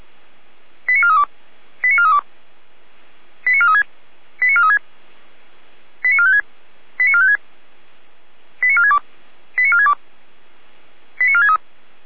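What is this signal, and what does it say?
Five-tone selective call (ZVEI), heard as quick runs of five tones, tones of about 70 ms. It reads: 89432 89432 89436 89436 89456 89456 89462 89462 89463
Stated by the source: Received on an SDR from server audio.f85360.feuerwehr-orlamuende.org (Feuerwehr Orlamünde fire brigade).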